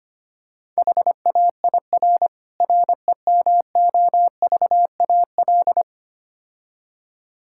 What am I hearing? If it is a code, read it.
Morse code, heard as a keyed tone, about 25 wpm, 700 Hz. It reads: HAIR REMOVAL
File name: practice / 25wpm / Difficult Words - Speed-Racing with Hints - 25wpm